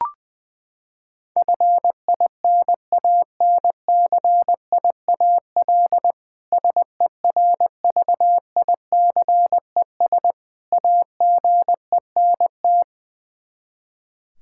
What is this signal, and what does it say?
Morse code, heard as a keyed tone, about 20 wpm, 700 Hz. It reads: FINANCIAL SERVICES AGENT
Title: Practice QSO - Exchange 3 - More Info - 20wpm